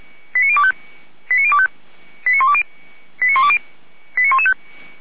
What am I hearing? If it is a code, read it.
Five-tone selective call (ZVEI), heard as quick runs of five tones, tones of about 70 ms. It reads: 89025 89025 89120 89120 89115
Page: /455/SDR/ZVEI/